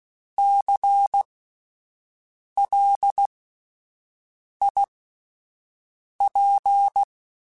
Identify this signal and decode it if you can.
Morse code, heard as a keyed tone, about 16 wpm, spaced as for 5 wpm, 800 Hz. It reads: CLIP
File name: Spaced Alphas / Spaced Alphas mp3s / Cs sp @10